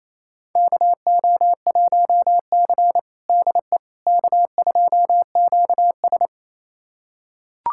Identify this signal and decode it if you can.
Morse code, heard as a keyed tone, about 28 wpm, 690 Hz. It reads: KO1C DE K2QS
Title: USA Callsign DE Callsign - 4 Characters - 28wpm